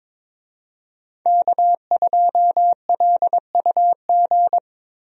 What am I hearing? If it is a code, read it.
Morse code, heard as a keyed tone, about 22 wpm, 700 Hz. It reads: K2LUG
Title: USA Call Signs - 5 Characters - Repeated 3x - 22wpm